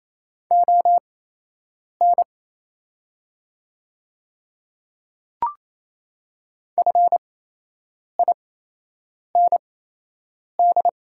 Morse code, heard as a keyed tone, about 28 wpm, spaced as for 8 wpm, 700 Hz. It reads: ON FIND
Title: 8x Character Spacing - Top 100 Words - 28wpm